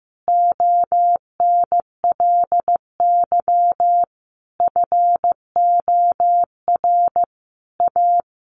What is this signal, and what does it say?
Morse code, heard as a keyed tone, about 15 wpm, 700 Hz. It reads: ONLY FOR A
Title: Sentences from Top 1000 Words 15wpm